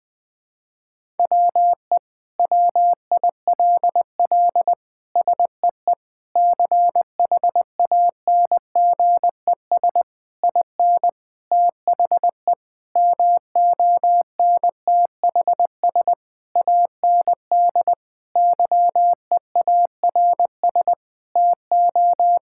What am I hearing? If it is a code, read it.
Morse code, heard as a keyed tone, about 21 wpm, 700 Hz. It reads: WE WILL SEE CHANGES IN THE MONTHS AND YEARS TO
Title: Sentences with 7-Word Phrases - 20wpm